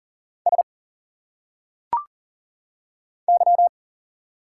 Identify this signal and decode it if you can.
Morse code, heard as a keyed tone, about 40 wpm, 700 Hz. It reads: S Y
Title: Single Letters - Phonetic Pronunciation 40wpm